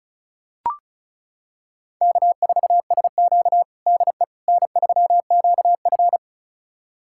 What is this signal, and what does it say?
Morse code, heard as a keyed tone, about 36 wpm, 700 Hz. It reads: K4SQ DE N3QF
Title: USA Callsign DE Callsign - 4 Characters - 35wpm